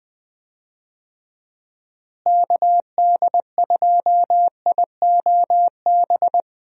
Morse code, heard as a keyed tone, about 20 wpm, 700 Hz. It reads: KD2IOB
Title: USA Call Signs - 6 Characters - Repeated 3x - 20wpm